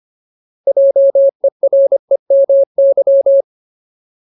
Morse code, heard as a keyed tone, about 25 wpm, 550 Hz. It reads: JEREMY
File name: Ultimate Word List / 25WPM Ultimate